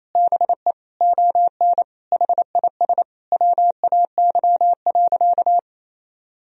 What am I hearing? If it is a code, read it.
Morse code, heard as a keyed tone, about 27 wpm, 700 Hz. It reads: BE ON HIS WAY.
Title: Sentences from Top 300 Words and Abbreviations 28wpm